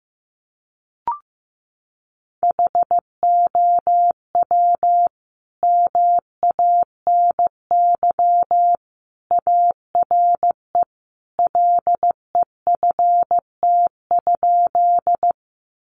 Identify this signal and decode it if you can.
Morse code, heard as a keyed tone, about 15 wpm, 700 Hz. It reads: HOW MANY ARE LEFT?
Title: Sentences from Top 300 Words 15wpm